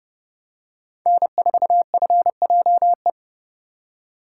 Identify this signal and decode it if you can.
Morse code, heard as a keyed tone, about 28 wpm, 700 Hz. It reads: N4FJE